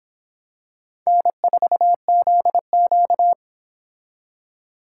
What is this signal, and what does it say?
Morse code, heard as a keyed tone, about 26 wpm, 700 Hz. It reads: N4ZQ